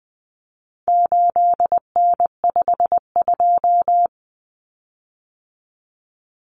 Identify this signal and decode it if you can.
Morse code, heard as a keyed tone, about 20 wpm, 700 Hz. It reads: 8N52
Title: Common Cut Numbers - 4-Digit 20wpm